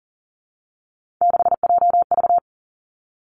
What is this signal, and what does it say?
Morse code, heard as a keyed tone, about 38 wpm, 700 Hz. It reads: 6JV
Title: Three Challenging Letter-Number Combinations 40wpm